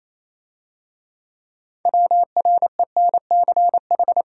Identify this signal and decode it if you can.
Morse code, heard as a keyed tone, about 27 wpm, 700 Hz. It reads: WRENCH
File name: Easy Words - Speed-Racing with Hints - 22wpm